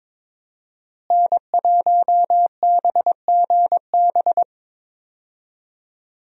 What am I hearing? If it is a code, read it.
Morse code, heard as a keyed tone, about 21 wpm, 700 Hz. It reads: N1BGB
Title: USA Call Signs - 5 Characters - Repeated 2x - 22wpm